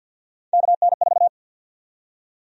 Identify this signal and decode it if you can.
Morse code, heard as a keyed tone, about 53 wpm, 700 Hz. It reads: KN4
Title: USA Call Sign Prefixes - 50wpm